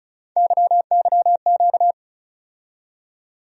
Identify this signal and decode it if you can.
Morse code, heard as a keyed tone, about 34 wpm, 700 Hz. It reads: YYQ
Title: USA Call Sign Postfixes - 3 Letters 35wpm